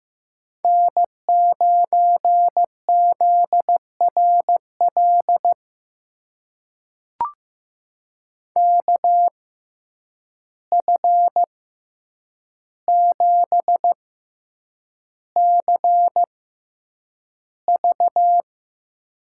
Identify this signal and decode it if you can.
Morse code, heard as a keyed tone, about 15 wpm, spaced as for 8 wpm, 700 Hz. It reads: N9ZRL KF7CV
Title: USA Call Signs - 5 Characters - 6x Character Spacing - 15wpm